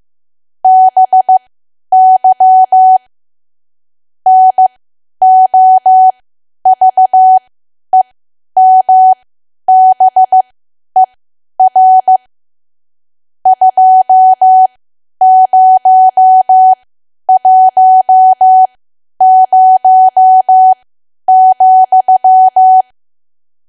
Morse code, heard as a keyed tone, about 15 wpm, spaced as for 10 wpm, 750 Hz. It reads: BY NOVEMBER 2010,
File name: cwa / W1AW-Level2 / 10WPM